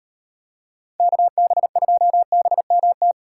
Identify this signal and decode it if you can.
Morse code, heard as a keyed tone, about 31 wpm, 700 Hz. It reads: KB2BMT
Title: Speed Racing - USA 2x3 Call Signs 25wpm